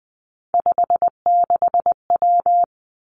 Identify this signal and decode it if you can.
Morse code, heard as a keyed tone, about 20 wpm, 700 Hz. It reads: 56W
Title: Three Challenging Letter-Number Combinations 20wpm